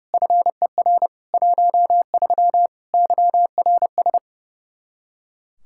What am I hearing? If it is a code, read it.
Morse code, heard as a keyed tone, about 32 wpm, 700 Hz. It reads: FER 13 YRS
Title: Practice QSO - Exchange 3 - More Info - 30wpm